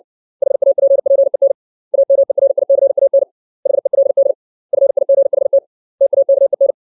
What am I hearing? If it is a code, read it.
Morse code, heard as a keyed tone, about 60 wpm, 550 Hz. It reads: HAPPEN AMERICAN HELD FIGHT TAKEN
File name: 632 words / 632 Words 60WPM